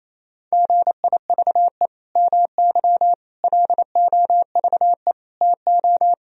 Morse code, heard as a keyed tone, about 28 wpm, 700 Hz. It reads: GIVE MY LOVE TO
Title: Sentences from Top 1000 Words - Review Part 4 - 28wpm